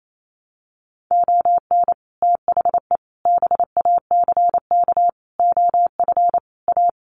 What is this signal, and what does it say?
Morse code, heard as a keyed tone, about 28 wpm, 700 Hz. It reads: ON THE BACK OF A
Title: Sets of 5 Words Encore2 28wpm